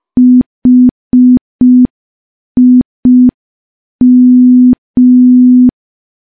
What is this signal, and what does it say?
Morse code, heard as a keyed tone, about 5 wpm, 250 Hz. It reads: HIM
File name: morse / morse-training-05WPM